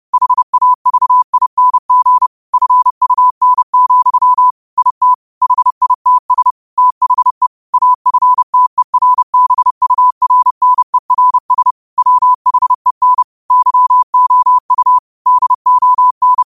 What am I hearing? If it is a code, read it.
Morse code, heard as a keyed tone, about 30 wpm, 1000 Hz. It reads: HAVING FUN, IT HITS THE AFTERBURNERS WHEN YOU DON
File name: qotdfiles / 30wpm